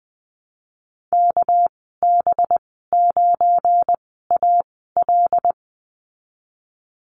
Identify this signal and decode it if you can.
Morse code, heard as a keyed tone, about 20 wpm, spaced as for 15 wpm, 700 Hz. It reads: KB9AL
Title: 200 Lesson 32g - Letter k - Call Signs 20wpm